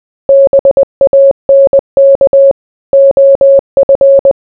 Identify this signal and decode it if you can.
Morse code, heard as a keyed tone, about 20 wpm, 550 Hz. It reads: BANK OF